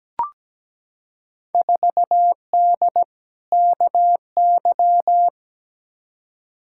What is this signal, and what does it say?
Morse code, heard as a keyed tone, about 17 wpm, 700 Hz. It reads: 4D KY
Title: Field Day Exchanges Encore - 17wpm